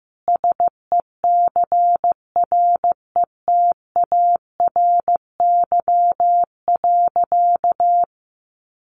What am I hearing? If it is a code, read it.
Morse code, heard as a keyed tone, about 15 wpm, 700 Hz. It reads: SECRETARY.
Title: Sentences from Top 1300 Words 15wpm